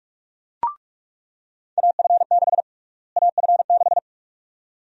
Morse code, heard as a keyed tone, about 43 wpm, 700 Hz. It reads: AF6 AF6
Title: USA Call Sign Prefixes - Repeated 2x - 45wpm